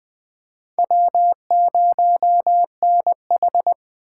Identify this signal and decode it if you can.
Morse code, heard as a keyed tone, about 20 wpm, 700 Hz. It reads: W0NH
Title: Speed Racing - USA 1x2 Call Signs 17wpm